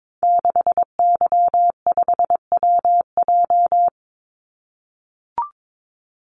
Morse code, heard as a keyed tone, about 22 wpm, 700 Hz. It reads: 6Y5WJ
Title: Call Signs - 22wpm